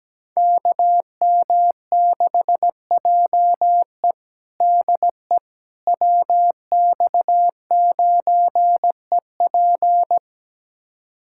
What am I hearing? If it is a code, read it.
Morse code, heard as a keyed tone, about 17 wpm, 700 Hz. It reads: KM6JE DE WX9EP
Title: USA Callsign DE Callsign 17wpm